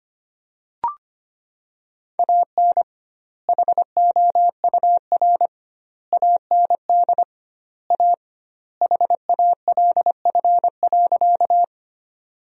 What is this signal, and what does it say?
Morse code, heard as a keyed tone, about 25 wpm, 700 Hz. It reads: AN HOUR AND A HALF.
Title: 2x Word-Spacing - Sentences from Top 400 Words - 25wpm